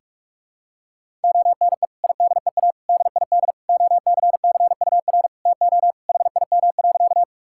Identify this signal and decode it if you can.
Morse code, heard as a keyed tone, about 46 wpm, 700 Hz. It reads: ONE IDEA DID OCCUR TO HIM.
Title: Sentences from Top 1000 Words - Review Part 6 - 45wpm